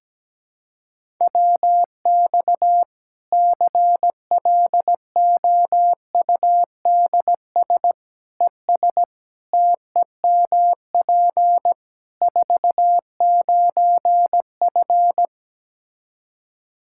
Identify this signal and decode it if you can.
Morse code, heard as a keyed tone, about 17 wpm, 700 Hz. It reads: WX CLOUDS ES TEMP 49F
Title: K1USN SST QSO Practice 17wpm